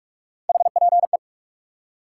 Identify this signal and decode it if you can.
Morse code, heard as a keyed tone, about 46 wpm, 700 Hz. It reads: HPE
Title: Top 90 QSO Elements 45wpm